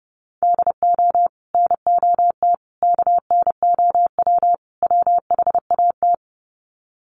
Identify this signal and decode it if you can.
Morse code, heard as a keyed tone, about 28 wpm, 700 Hz. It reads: DO NOT KNOW WHAT
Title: Sets of 4 Words 30wpm